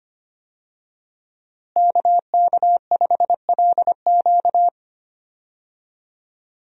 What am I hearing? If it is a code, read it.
Morse code, heard as a keyed tone, about 25 wpm, 700 Hz. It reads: KK5LQ
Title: USA Call Signs - 5 Characters - Repeated 2x - 25wpm